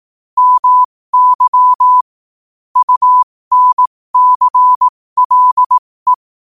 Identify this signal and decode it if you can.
Morse code, heard as a keyed tone, about 18 wpm, spaced as for 15 wpm, 1000 Hz. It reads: MY UNCLE
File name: qotdfiles / 15wpm